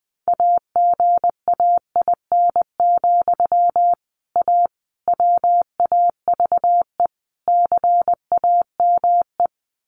Morse code, heard as a keyed tone, about 20 wpm, 700 Hz. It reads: AGAIN, A WAVE CAME
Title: Sentences from Top 1400 Words 20wpm